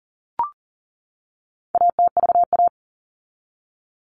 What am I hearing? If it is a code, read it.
Morse code, heard as a keyed tone, about 40 wpm, 700 Hz. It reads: ATVA
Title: All Cut Numbers - 4-Digit 40wpm